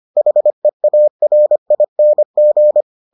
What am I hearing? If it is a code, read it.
Morse code, heard as a keyed tone, about 25 wpm, 600 Hz. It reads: HEARING